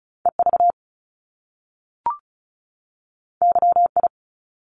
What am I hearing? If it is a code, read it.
Morse code, heard as a keyed tone, about 36 wpm, 700 Hz. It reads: EV YI
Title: Top Two Letter Combinations in Words 35wpm